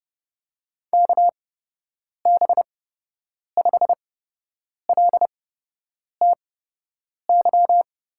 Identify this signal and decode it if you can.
Morse code, heard as a keyed tone, about 30 wpm, spaced as for 8 wpm, 700 Hz. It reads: KB5LTY
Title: USA Call Signs - 6 Characters - 8x Character Spacing - 30wpm